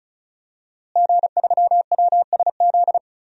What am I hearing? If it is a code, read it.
Morse code, heard as a keyed tone, about 34 wpm, 700 Hz. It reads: G3WSZ
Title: DXCC-100 Call Signs - 5 Characters - Repeated 3x - 35wpm